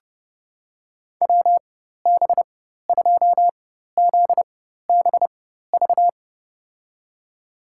Morse code, heard as a keyed tone, about 30 wpm, spaced as for 14 wpm, 700 Hz. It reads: WB2ZBV
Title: USA Call Signs - 6 Characters - 4x Character Spacing - 30wpm